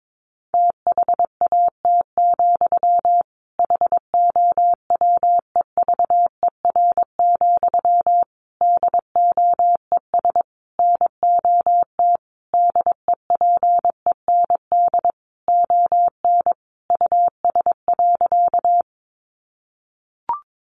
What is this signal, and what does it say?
Morse code, heard as a keyed tone, about 22 wpm, 700 Hz. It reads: THAT, HOWEVER, DOES NOT DEPEND ON US.
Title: Sentences from Top 1300 Words 22wpm